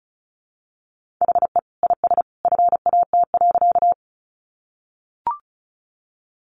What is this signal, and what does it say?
Morse code, heard as a keyed tone, about 34 wpm, 700 Hz. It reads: HE IS FAT.
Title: Easy Phrases 35wpm